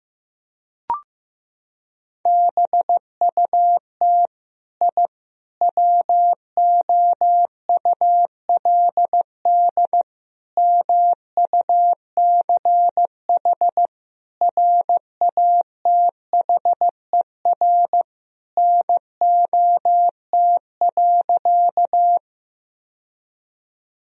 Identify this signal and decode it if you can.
Morse code, heard as a keyed tone, about 15 wpm, 700 Hz. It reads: BUT I WOULD MUCH RATHER NOT.
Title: Sentences from Top 1000 Words - Review Part 4 - 15wpm